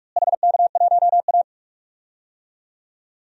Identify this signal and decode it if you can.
Morse code, heard as a keyed tone, about 41 wpm, 700 Hz. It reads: HK1A
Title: DXCC-100 Call Signs - 4 Characters - Repeated 2x - 45wpm